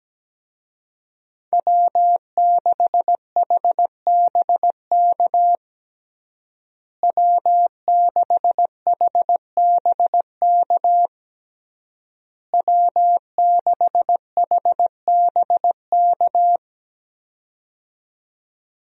Morse code, heard as a keyed tone, about 17 wpm, 700 Hz. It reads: W6HBK W6HBK W6HBK
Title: USA Call Signs - 5 Characters - Repeated 3x - 17wpm